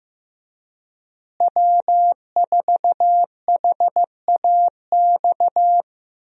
Morse code, heard as a keyed tone, about 15 wpm, 700 Hz. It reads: W4HAX
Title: USA Call Signs - 5 Characters - Repeated 3x - 15wpm